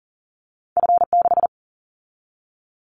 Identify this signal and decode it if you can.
Morse code, heard as a keyed tone, about 40 wpm, 700 Hz. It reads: F6
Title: Callsign Prefixes for Top 50 DXCC - 40wpm